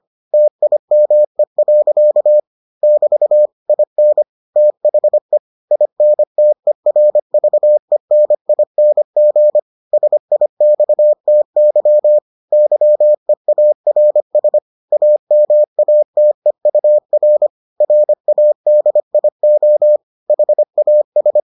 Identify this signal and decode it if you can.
Morse code, heard as a keyed tone, about 25 wpm, 600 Hz. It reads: TIME. <BT> IN THE INTERVENING SIXTY YEARS AMATEUR RADIO HAS